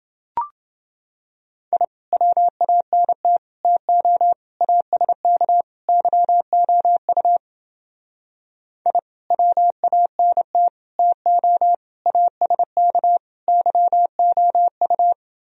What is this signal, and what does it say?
Morse code, heard as a keyed tone, about 29 wpm, 700 Hz. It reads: I WANT TO ASK YOU I WANT TO ASK YOU